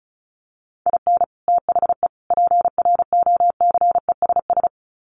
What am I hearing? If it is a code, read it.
Morse code, heard as a keyed tone, about 34 wpm, 700 Hz. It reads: IN THE PROCESS
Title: Sets of 3 Words 35wpm